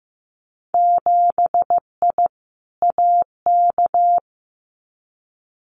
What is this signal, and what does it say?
Morse code, heard as a keyed tone, about 15 wpm, 700 Hz. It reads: 7I AK
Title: Winter Field Day Exchanges Encore 15wpm